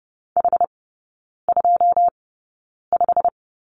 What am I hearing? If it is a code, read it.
Morse code, heard as a keyed tone, about 30 wpm, spaced as for 9 wpm, 690 Hz. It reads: H25
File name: Lesson 16e - Letter c - Three Characters 30wpm